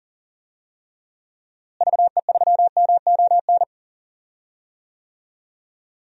Morse code, heard as a keyed tone, about 40 wpm, 700 Hz. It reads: VE3MON
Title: DXCC-100 Call Signs - 6 Characters - Repeated 3x - 40wpm